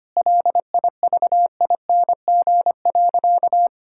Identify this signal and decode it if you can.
Morse code, heard as a keyed tone, about 25 wpm, 700 Hz. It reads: LIVING.
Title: Sentences from Top 1000 Words - Review Part 6 - 25wpm